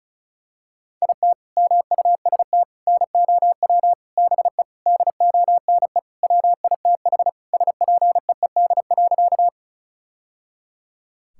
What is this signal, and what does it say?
Morse code, heard as a keyed tone, about 36 wpm, 700 Hz. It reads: IT MUST NOW BE DONE WITH SPEED.
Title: Sentences from Top 1000 Words - Review Part 5 - 35wpm